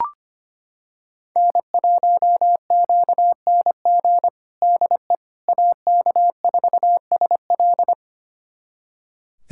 Morse code, heard as a keyed tone, about 25 wpm, 700 Hz. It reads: N1QNG DE AK4SL
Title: USA Callsign DE Callsign - 5 Characters - 25wpm